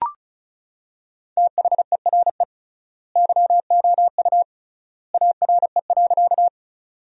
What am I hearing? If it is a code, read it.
Morse code, heard as a keyed tone, about 36 wpm, 700 Hz. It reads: THERE YOU ARE.
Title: 3x Word-Spacing - Sentences from Top 100 Words - 35wpm